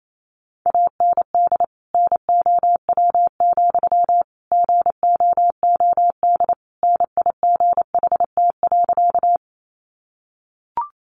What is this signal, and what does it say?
Morse code, heard as a keyed tone, about 28 wpm, 700 Hz. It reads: AND NOW, GOOD NIGHT.
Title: Sentences from Top 300 Words 28wpm